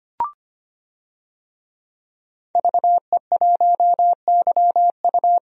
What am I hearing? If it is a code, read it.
Morse code, heard as a keyed tone, about 25 wpm, 690 Hz. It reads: VE1YU
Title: DXCC-100 Call Signs - 5 Characters - 25wpm